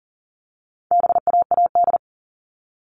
Morse code, heard as a keyed tone, about 40 wpm, 700 Hz. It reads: BAAD
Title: All Cut Numbers - 4-Digit 40wpm